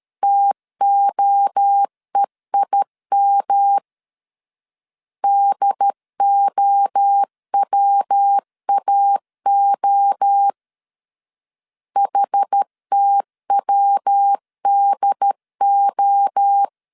Morse code, heard as a keyed tone, about 13 wpm, 800 Hz. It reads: TOEIM DOWAO HTWDO
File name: CW / DET / 1 3 1500C S B 12wpm 2828